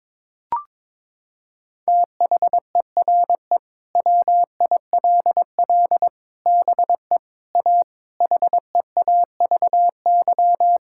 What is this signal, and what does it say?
Morse code, heard as a keyed tone, about 22 wpm, 700 Hz. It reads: THERE WILL BE A HEAVY
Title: Sentences from Top 1600 Words 22wpm